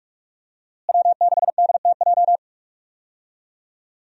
Morse code, heard as a keyed tone, about 42 wpm, 700 Hz. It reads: W6DTJ
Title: USA Call Signs - 5 Characters - Repeated 2x - 45wpm